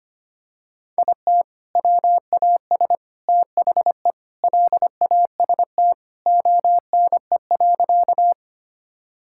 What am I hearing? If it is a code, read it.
Morse code, heard as a keyed tone, about 25 wpm, 700 Hz. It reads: IT WAS THE LAST ONE.
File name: practice / 25wpm / Sentences with 3-Word Phrases - 25wpm